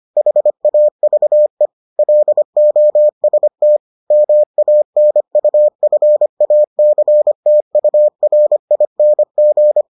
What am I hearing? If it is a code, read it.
Morse code, heard as a keyed tone, about 25 wpm, 600 Hz. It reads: HAVE LOST MANUFACTURING